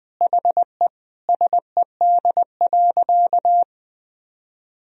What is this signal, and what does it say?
Morse code, heard as a keyed tone, about 21 wpm, 700 Hz. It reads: HE SED.
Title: Sentences from Top 200 Words and Abbreviations 20wpm